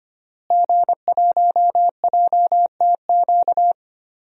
Morse code, heard as a keyed tone, about 25 wpm, 700 Hz. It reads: G1JTQ
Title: DXCC-100 Call Signs - 5 Characters - Repeated 3x - 25wpm